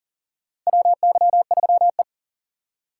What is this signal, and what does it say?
Morse code, heard as a keyed tone, about 40 wpm, 700 Hz. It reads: WY3E
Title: USA Call Signs - 2x Character Spacing - 40wpm